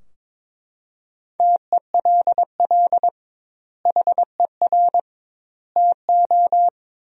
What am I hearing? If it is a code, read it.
Morse code, heard as a keyed tone, about 22 wpm, 700 Hz. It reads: TELL HER TO
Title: 2x Word-Spacing - Sentences from Top 300 Words - 22wpm